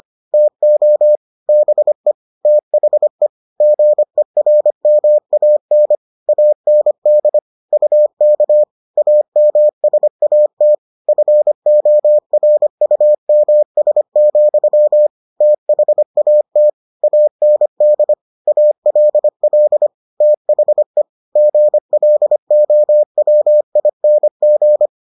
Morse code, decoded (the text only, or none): TO BE THE GERMAN AND UK AMSAT FORUMS, THAT AND ALL THE GLOWING